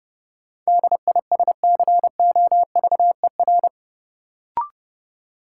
Morse code, heard as a keyed tone, about 31 wpm, 700 Hz. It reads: DISCOVER
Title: Sentences from Top 1100 Words - New Word Drill Alt - 30wpm